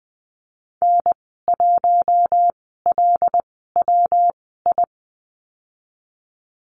Morse code, heard as a keyed tone, about 20 wpm, spaced as for 15 wpm, 700 Hz. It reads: N1LWI
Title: 128 Lesson 21g - Number 6 - Call Signs 20wpm